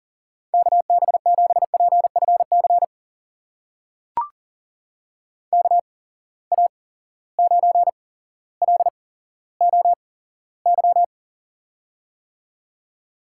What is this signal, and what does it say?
Morse code, heard as a keyed tone, about 39 wpm, spaced as for 17 wpm, 700 Hz. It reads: KB7PFC KA9LOY
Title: USA Call Signs - 6 Characters - 8x Character Spacing - 40wpm